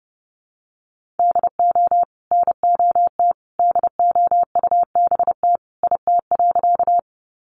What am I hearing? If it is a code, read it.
Morse code, heard as a keyed tone, about 30 wpm, 700 Hz. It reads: DO NOT DOUBT IT.
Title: Sentences from Top 1500 Words 30wpm